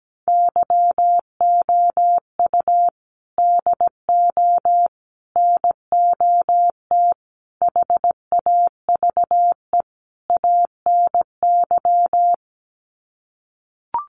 Morse code, heard as a keyed tone, about 17 wpm, 700 Hz. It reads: YOU DO NOT HAVE ANY
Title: Sets of 5 Words Encore 17wpm